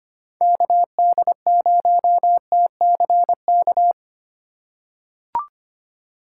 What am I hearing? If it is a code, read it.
Morse code, heard as a keyed tone, about 25 wpm, 700 Hz. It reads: KD0TCK
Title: USA Call Signs - 6 Characters - 3x Character Spacing - 25wpm